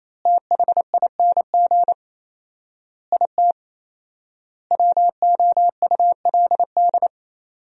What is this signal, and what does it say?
Morse code, heard as a keyed tone, about 28 wpm, 700 Hz. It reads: THING IT WOULD
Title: 4x Word-Spacing - Sentences from Top 200 Words - 28wpm